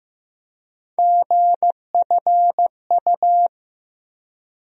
Morse code, heard as a keyed tone, about 15 wpm, 700 Hz. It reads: GFU
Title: USA Call Sign Postfixes - 3 Letters 15wpm